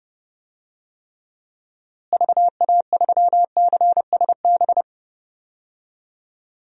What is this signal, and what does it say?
Morse code, heard as a keyed tone, about 32 wpm, 700 Hz. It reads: VA3CSB
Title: DXCC-100 Call Signs - 6 Characters - Repeated 3x - 30wpm